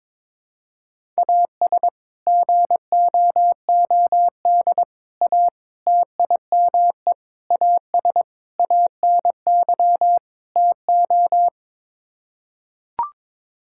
Morse code, heard as a keyed tone, about 21 wpm, 700 Hz. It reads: AS GOOD A TIME AS ANY TO
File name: practice / 22wpm / Sets of 7 Words - 22wpm